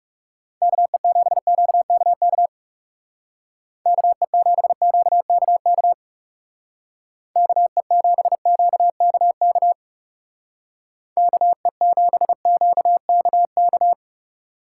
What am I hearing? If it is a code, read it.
Morse code, heard as a keyed tone, about 39 wpm, 700 Hz. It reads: KE7QKK KE7QKK KE7QKK KE7QKK